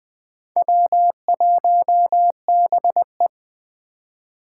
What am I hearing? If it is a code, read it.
Morse code, heard as a keyed tone, about 20 wpm, 700 Hz. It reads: W1BE
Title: Speed Racing - USA 1x2 Call Signs 17wpm